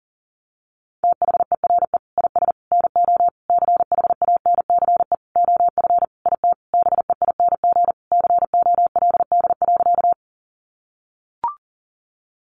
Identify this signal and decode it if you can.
Morse code, heard as a keyed tone, about 40 wpm, 700 Hz. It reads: THERE IS NO CHANCE OF IT BEING COLD.
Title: Sentences from Top 800 Words 40wpm